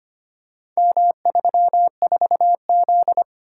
Morse code, heard as a keyed tone, about 25 wpm, 700 Hz. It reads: M34Z
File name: practice / 25wpm / Four Letter-Number Combinations - 25wpm